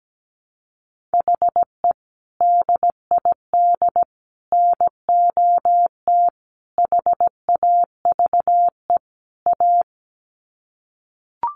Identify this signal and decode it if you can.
Morse code, heard as a keyed tone, about 17 wpm, 700 Hz. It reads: HE DID NOT HAVE A
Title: Sets of 5 Words Encore 17wpm